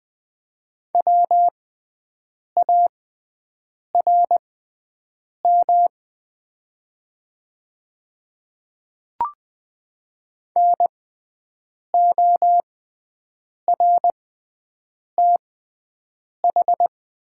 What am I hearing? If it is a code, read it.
Morse code, heard as a keyed tone, about 20 wpm, spaced as for 7 wpm, 700 Hz. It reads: WARM NORTH
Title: 6x Character Spacing - Top 500 Words - 20wpm